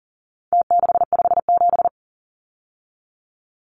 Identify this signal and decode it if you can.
Morse code, heard as a keyed tone, about 40 wpm, 700 Hz. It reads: T657
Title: Numbers - Contest Style - 4-Digit 40wpm